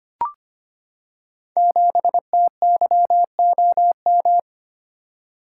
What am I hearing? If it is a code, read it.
Morse code, heard as a keyed tone, about 25 wpm, 700 Hz. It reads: 7TYOM
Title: Code Groups - 5 Letter:Numbers - 25wpm